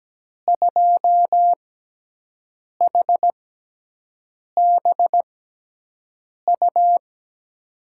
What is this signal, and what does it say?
Morse code, heard as a keyed tone, about 17 wpm, spaced as for 6 wpm, 700 Hz. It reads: 2HBU
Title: USA Call Signs - 6x Character Spacing - 17wpm